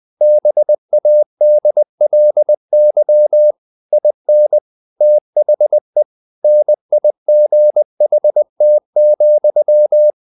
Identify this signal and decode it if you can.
Morse code, heard as a keyed tone, about 20 wpm, 600 Hz. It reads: BADLY IN THE NIGHT,
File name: the old man and the sea part 2 20wpm